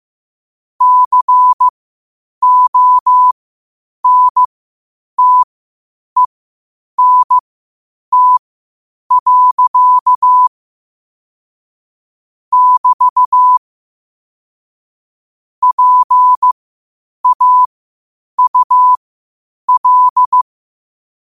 Morse code, heard as a keyed tone, about 15 wpm, spaced as for 8 wpm, 1000 Hz. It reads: CONTENT. <BT> PAUL
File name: qotdfiles / 8wpm